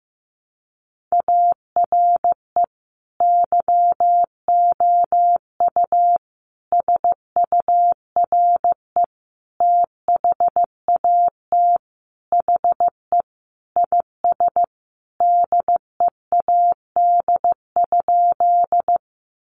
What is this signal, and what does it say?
Morse code, heard as a keyed tone, about 15 wpm, 700 Hz. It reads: ARE YOU SURE THAT HE IS DEAD?
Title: Sentences from Top 1500 Words 15wpm